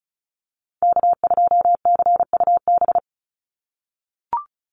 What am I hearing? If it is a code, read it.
Morse code, heard as a keyed tone, about 34 wpm, 700 Hz. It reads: K2CUB
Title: Call Signs - 35wpm